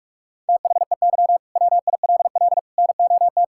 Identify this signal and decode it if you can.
Morse code, heard as a keyed tone, about 46 wpm, 700 Hz. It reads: THEY WILL NOT
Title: Sentences from Top 1000 Words - Review Part 5 - 45wpm